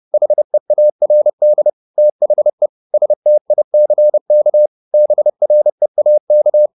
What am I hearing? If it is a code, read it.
Morse code, heard as a keyed tone, about 30 wpm, 600 Hz. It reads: HEARD THE STICK BREAK
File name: the old man and the sea / the old man and the sea 1 30wpm